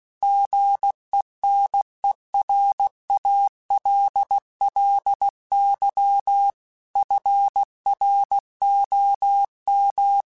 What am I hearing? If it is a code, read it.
Morse code, heard as a keyed tone, about 16 wpm, 800 Hz. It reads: GENERALLY FROM
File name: Weather Sets / Weather Sets-mp3s / Weather 4 10 @15